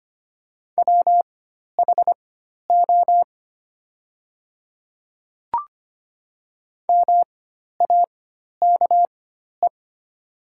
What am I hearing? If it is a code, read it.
Morse code, heard as a keyed tone, about 25 wpm, spaced as for 12 wpm, 700 Hz. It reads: WHO MAKE
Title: 4x Character Spacing - Top 100 Words - 25wpm